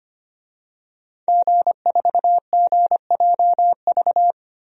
Morse code, heard as a keyed tone, about 25 wpm, 700 Hz. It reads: G4GJV
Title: DXCC-100 Call Signs - 5 Characters - Repeated 3x - 25wpm